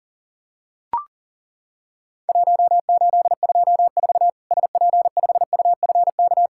Morse code, heard as a keyed tone, about 40 wpm, 700 Hz. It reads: 1824 SP5UFK
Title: Worked All Europe DX Contest - QTC Exchange - 40wpm